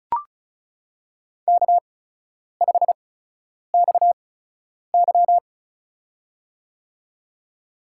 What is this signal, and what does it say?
Morse code, heard as a keyed tone, about 34 wpm, spaced as for 10 wpm, 700 Hz. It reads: K5XY
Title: USA Call Signs - 4 Characters - 8x Character Spacing - 35wpm